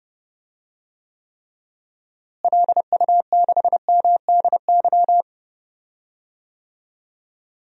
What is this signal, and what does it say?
Morse code, heard as a keyed tone, about 30 wpm, 700 Hz. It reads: LU6MDY
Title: DXCC-100 Call Signs - 6 Characters - Repeated 3x - 30wpm